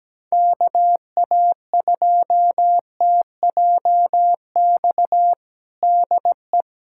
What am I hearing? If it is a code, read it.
Morse code, heard as a keyed tone, about 17 wpm, 700 Hz. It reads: KA2TJX DE
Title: USA Callsign DE Callsign 17wpm